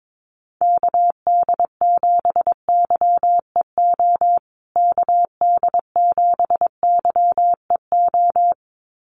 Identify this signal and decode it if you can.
Morse code, heard as a keyed tone, about 22 wpm, 700 Hz. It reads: KD7YEO KD7YEO
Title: USA Call Signs Repeated 2x 22wpm